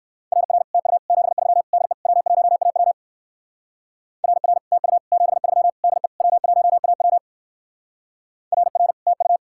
Morse code, heard as a keyed tone, about 55 wpm, 700 Hz. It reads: RR TU 73 DE W1AW RR TU 73 DE W1AW RR TU
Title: Speed-Racing Common QSO Phrases - 45wpm